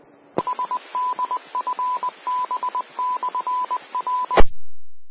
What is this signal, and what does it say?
Morse code, heard as a keyed tone, about 20 wpm, 1000 Hz, with radio noise. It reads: SDFB/R
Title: Austin, TX Amateur Repeaters